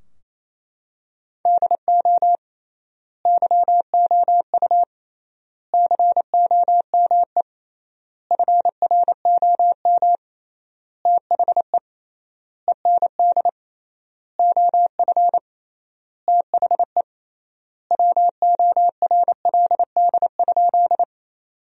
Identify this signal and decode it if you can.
Morse code, heard as a keyed tone, about 28 wpm, 700 Hz. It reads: DO YOU COME FROM THE END OF THE WORLD?
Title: 3x Word-Spacing - Sentences from Top 200 Words - 28wpm